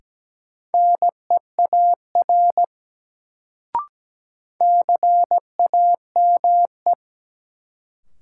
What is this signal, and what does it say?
Morse code, heard as a keyed tone, about 17 wpm, 700 Hz. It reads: NEAR CAME
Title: Sentences from Top 200 Words - New Word Drill - 17wpm